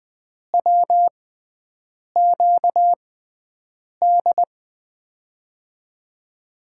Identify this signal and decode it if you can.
Morse code, heard as a keyed tone, about 20 wpm, spaced as for 7 wpm, 700 Hz. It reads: WQD
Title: USA Call Signs - 6x Character Spacing - 20wpm